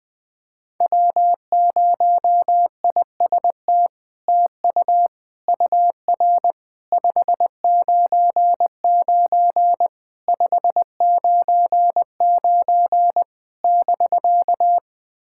Morse code, heard as a keyed tone, about 20 wpm, 700 Hz. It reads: W0IST TU UR 599 599 <BK>
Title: SOTA QSO Practice 20wpm